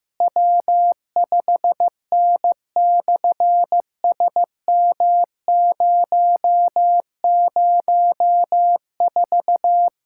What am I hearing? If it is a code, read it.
Morse code, heard as a keyed tone, about 14 wpm, 700 Hz. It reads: W5N/SM004
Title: SOTA QSO Practice 15wpm